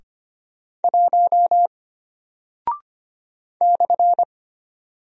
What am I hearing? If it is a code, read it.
Morse code, heard as a keyed tone, about 25 wpm, 700 Hz. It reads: 1 /
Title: Single Character (Letter:Number:Punctuation:Prosign) Alt 25wpm